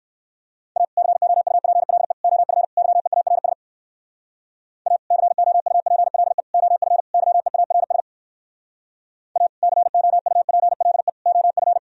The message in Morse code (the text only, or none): A COUPLE OF YEARS A COUPLE OF YEARS A COUPLE OF